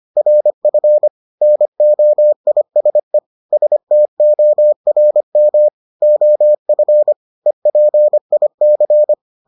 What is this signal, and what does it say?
Morse code, heard as a keyed tone, about 25 wpm, 600 Hz. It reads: RF NOISE STORM OF EPIC